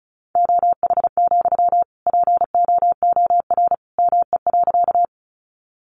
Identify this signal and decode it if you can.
Morse code, heard as a keyed tone, about 34 wpm, 700 Hz. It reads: OH, POOR ME.
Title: Sentences from Top 1000 Words 35wpm